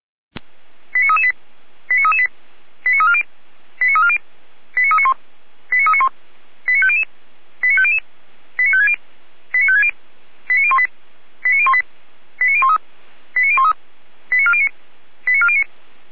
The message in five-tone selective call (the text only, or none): 89338 89338 89340 89340 89391 89391 89500 89500 89560 89560 89018 89018 89013 89013 89409 89409